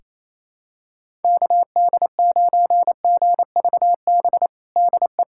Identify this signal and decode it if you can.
Morse code, heard as a keyed tone, about 28 wpm, 700 Hz. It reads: KD9GVB DE
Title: USA Callsign DE Callsign - 6 Characters - 28wpm